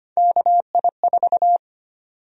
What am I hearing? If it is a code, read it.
Morse code, heard as a keyed tone, about 25 wpm, 700 Hz. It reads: KI4
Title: USA Call Sign Prefixes - 25wpm